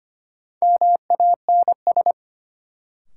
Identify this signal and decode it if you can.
Morse code, heard as a keyed tone, about 25 wpm, 700 Hz. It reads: MANS
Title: Sentences from Top 1000 Words - Review Part 6 - New Word Drill - 25wpm